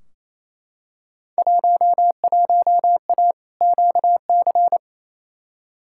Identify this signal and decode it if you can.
Morse code, heard as a keyed tone, about 28 wpm, 700 Hz. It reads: 11A QC
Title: Field Day Exchanges Encore - 28wpm